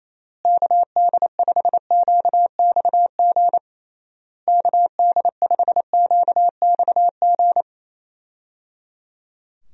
Morse code, heard as a keyed tone, about 28 wpm, 700 Hz. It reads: KD5QXG KD5QXG
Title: USA Call Signs - 6 Characters - Repeated 3x - 28wpm